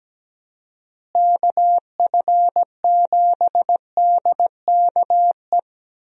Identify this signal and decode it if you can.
Morse code, heard as a keyed tone, about 17 wpm, 700 Hz. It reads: KF7DKE
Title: USA Call Signs - 6 Characters - Repeated 3x - 17wpm